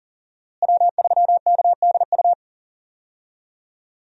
Morse code, heard as a keyed tone, about 40 wpm, 700 Hz. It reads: W3KDU
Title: USA Call Signs - 5 Characters - Repeated 3x - 40wpm